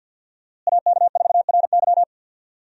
Morse code, heard as a keyed tone, about 52 wpm, 690 Hz. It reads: AK4RY
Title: USA Call Signs - 5 Characters - Repeated 3x - 50wpm